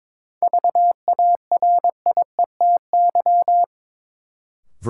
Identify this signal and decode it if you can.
Morse code, heard as a keyed tone, about 21 wpm, 700 Hz. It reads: VARIETY
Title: Sets of 3 Words - New Word Drill - 22wpm